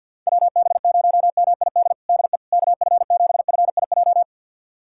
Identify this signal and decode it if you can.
Morse code, heard as a keyed tone, about 52 wpm, 700 Hz. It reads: WB0MID DE KF7FIJ
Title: USA Callsign DE Callsign - 6 Characters - 50wpm